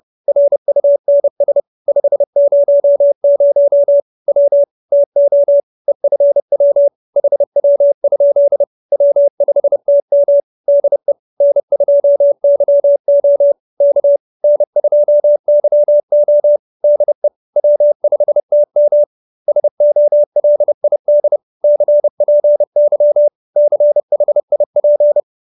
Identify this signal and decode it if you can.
Morse code, heard as a keyed tone, about 30 wpm, 570 Hz. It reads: RUNS 500 W TO EFW HW? W5TM DE N2YO K N2YO DE W5TM SOLID CPY CHIP
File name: QSO316 30